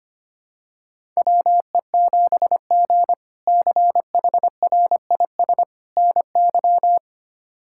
Morse code, heard as a keyed tone, about 25 wpm, 700 Hz. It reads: WE7G CHRIS NY
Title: K1USN SST QSO Practice - 25wpm